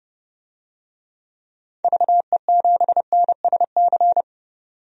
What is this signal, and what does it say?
Morse code, heard as a keyed tone, about 31 wpm, 700 Hz. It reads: VE7NSC
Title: DXCC-100 Call Signs - 6 Characters - 30wpm